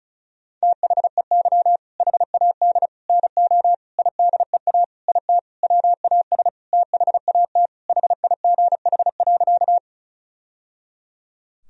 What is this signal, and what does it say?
Morse code, heard as a keyed tone, about 34 wpm, 700 Hz. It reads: THEY HAD NO IDEA IT WAS THAT HIGH.
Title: Sentences from Top 1000 Words - Review Part 1 - 35wpm